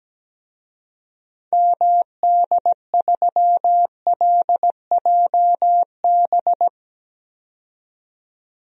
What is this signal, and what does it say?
Morse code, heard as a keyed tone, about 17 wpm, 700 Hz. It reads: MD3LJB
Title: DXCC-100 Call Signs - 6 Characters - 17wpm